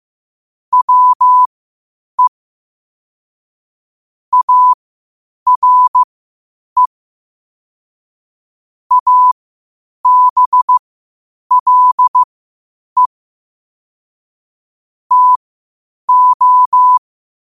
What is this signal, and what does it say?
Morse code, heard as a keyed tone, about 15 wpm, spaced as for 8 wpm, 1000 Hz. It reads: WE ARE ABLE TO
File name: qotdfiles / 8wpm